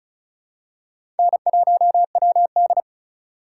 Morse code, heard as a keyed tone, about 36 wpm, 700 Hz. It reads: N1WD
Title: USA Call Signs - 4x Character Spacing - 35wpm